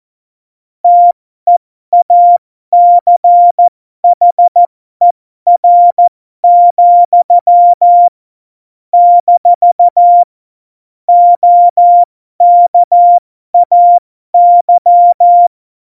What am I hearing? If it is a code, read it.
Morse code, heard as a keyed tone, about 14 wpm, spaced as for 12 wpm, 700 Hz. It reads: TEACHER, - OKAY
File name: Morsekurs / Witze